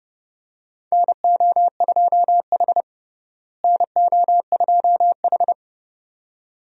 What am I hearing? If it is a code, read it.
Morse code, heard as a keyed tone, about 32 wpm, 700 Hz. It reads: NO2H NO2H
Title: USA Call Signs - 4 Characters - Repeated 2x - 30wpm